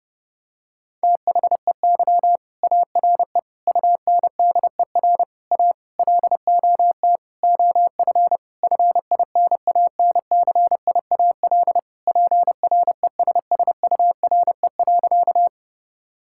Morse code, heard as a keyed tone, about 30 wpm, 700 Hz. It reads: THEY ARE UNDER A LOT OF FINANCIAL PRESSURE.
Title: Sentences from Top 1000 Words - Review Part 6 - 30wpm